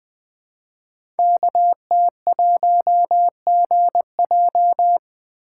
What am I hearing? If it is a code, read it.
Morse code, heard as a keyed tone, about 20 wpm, 700 Hz. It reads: KT1GJ
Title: Code Groups - 5 Letter:Numbers - 20wpm